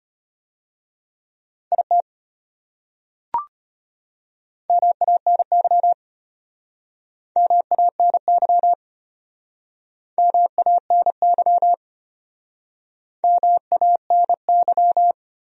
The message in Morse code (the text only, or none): IT MANY MANY MANY MANY